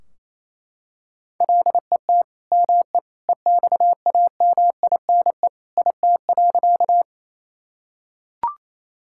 Morse code, heard as a keyed tone, about 28 wpm, 700 Hz. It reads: LET ME EXAMINE IT.
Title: Sentences from Top 1600 Words 28wpm